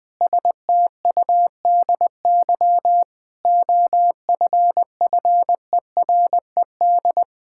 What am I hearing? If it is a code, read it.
Morse code, heard as a keyed tone, about 20 wpm, 700 Hz. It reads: STUDY OFFERED
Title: Sentences with 8-Word Phrases - 20wpm